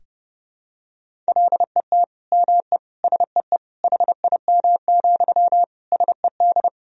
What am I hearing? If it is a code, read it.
Morse code, heard as a keyed tone, about 30 wpm, 700 Hz. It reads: LET ME SEE HIM, SED
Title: Sentences from Top 400 Words and Abbreviations 30wpm